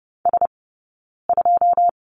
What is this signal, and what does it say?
Morse code, heard as a keyed tone, about 30 wpm, 700 Hz. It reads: S 2
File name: Lesson 19d - Letter w - Two Characters 30wpm